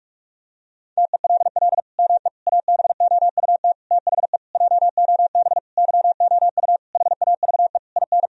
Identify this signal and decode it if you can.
Morse code, heard as a keyed tone, about 46 wpm, 700 Hz. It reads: TELL ME ABOUT THE JOB YOU HAVE IN
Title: Sentences from Top 1000 Words - Review Part 3 - 45wpm